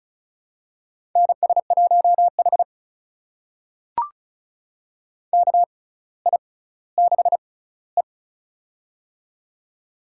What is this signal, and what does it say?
Morse code, heard as a keyed tone, about 36 wpm, spaced as for 18 wpm, 700 Hz. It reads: NS1H KI6E
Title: USA Call Signs - 4 Characters - 6x Character Spacing - 35wpm